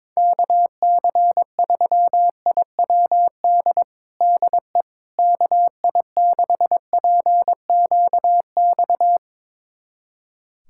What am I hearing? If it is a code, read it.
Morse code, heard as a keyed tone, about 22 wpm, 700 Hz. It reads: KC3IWD DE KI6PQX
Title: USA Callsign DE Callsign - 6 Characters - 22wpm